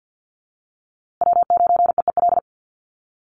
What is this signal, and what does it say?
Morse code, heard as a keyed tone, about 50 wpm, 700 Hz. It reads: W9EEL
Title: USA Call Signs Repeated 2x 50wpm